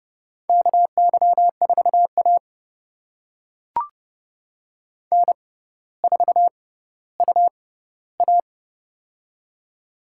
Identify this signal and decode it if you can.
Morse code, heard as a keyed tone, about 28 wpm, spaced as for 15 wpm, 700 Hz. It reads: KY4A N4UA
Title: USA Call Signs - 4 Characters - 6x Character Spacing - 30wpm